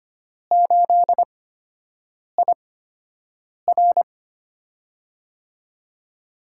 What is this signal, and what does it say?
Morse code, heard as a keyed tone, about 25 wpm, spaced as for 7 wpm, 700 Hz. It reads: 8IR
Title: USA Call Signs - 4 Characters - 8x Character Spacing - 25wpm